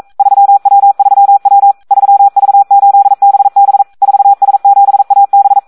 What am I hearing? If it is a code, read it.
Morse code, heard as a keyed tone, about 42 wpm, 800 Hz. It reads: 3W3W 3V8BB 4S7AB
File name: Audio practica CW / SD 1 42